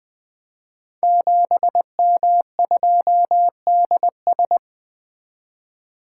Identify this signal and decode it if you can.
Morse code, heard as a keyed tone, about 20 wpm, 700 Hz. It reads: 7M2DS
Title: Code Groups - 5 Letter:Numbers - 20wpm